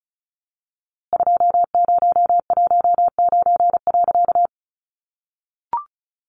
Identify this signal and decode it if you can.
Morse code, heard as a keyed tone, about 34 wpm, 700 Hz. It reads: 2019.
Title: A Sound Of Thunder 35wpm